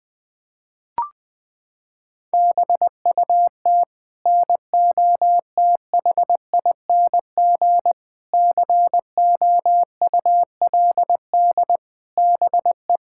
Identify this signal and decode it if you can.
Morse code, heard as a keyed tone, about 20 wpm, 700 Hz. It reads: BUT NOTHING COULD BE
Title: Sets of 8 Words - 20wpm